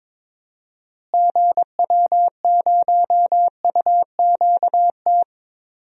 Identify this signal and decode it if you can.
Morse code, heard as a keyed tone, about 22 wpm, 700 Hz. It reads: GW0UQT
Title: DXCC-100 Call Signs - 6 Characters - Repeated 2x - 22wpm